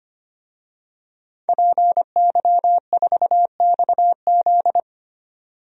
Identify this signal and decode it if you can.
Morse code, heard as a keyed tone, about 25 wpm, 700 Hz. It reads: PY4XZ
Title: DXCC-100 Call Signs - 5 Characters - 25wpm